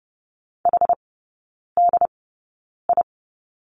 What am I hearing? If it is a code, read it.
Morse code, heard as a keyed tone, about 30 wpm, spaced as for 9 wpm, 700 Hz. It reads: HDI
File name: 081 Lesson 14e - Number 2 - Three Characters 30wpm